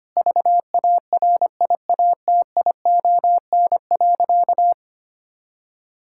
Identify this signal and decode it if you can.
Morse code, heard as a keyed tone, about 25 wpm, 700 Hz. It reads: VARIATION.
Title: Sentences from Top 1600 Words 25wpm